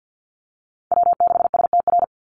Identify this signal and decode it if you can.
Morse code, heard as a keyed tone, about 50 wpm, 700 Hz. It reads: W6STR
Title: Call Signs Repeated 3x - 50wpm